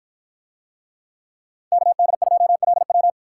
Morse code, heard as a keyed tone, about 49 wpm, 700 Hz. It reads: KD2LW